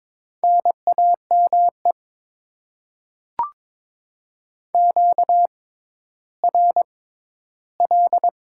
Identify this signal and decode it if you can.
Morse code, heard as a keyed tone, about 22 wpm, spaced as for 13 wpm, 700 Hz. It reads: NAME QRL
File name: Top 20 QSO Elements - 6x Character Spacing - 22wpm